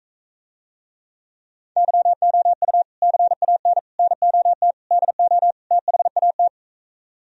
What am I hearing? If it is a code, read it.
Morse code, heard as a keyed tone, about 44 wpm, 700 Hz. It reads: YOU CAN NOT DO THAT